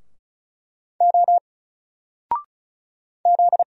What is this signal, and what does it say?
Morse code, heard as a keyed tone, about 36 wpm, 700 Hz. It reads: O Z
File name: Single-Letter-Number-35wpm